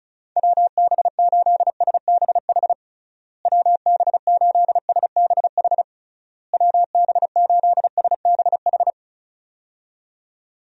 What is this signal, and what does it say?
Morse code, heard as a keyed tone, about 34 wpm, 700 Hz. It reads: WB8SBH WB8SBH WB8SBH
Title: USA Call Signs - 6 Characters - Repeated 3x - 35wpm